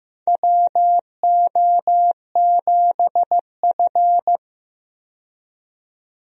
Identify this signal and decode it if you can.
Morse code, heard as a keyed tone, about 15 wpm, 700 Hz. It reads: WO7F
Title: USA Call Signs - 4 Characters - 15wpm